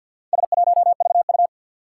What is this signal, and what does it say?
Morse code, heard as a keyed tone, about 49 wpm, 700 Hz. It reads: S1VU
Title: Four Challenging Letter-Number Combinations 50wpm